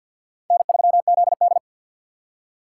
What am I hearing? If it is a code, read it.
Morse code, heard as a keyed tone, about 51 wpm, 700 Hz. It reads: N3ZD
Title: USA Call Signs - 4x Character Spacing - 50wpm